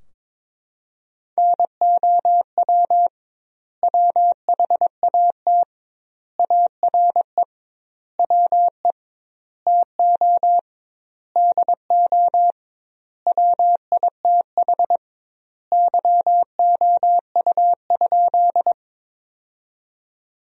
Morse code, heard as a keyed tone, about 22 wpm, 700 Hz. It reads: NOW WHAT ARE WE TO DO WITH YOU?
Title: 2x Word-Spacing - Sentences from Top 100 Words - 22wpm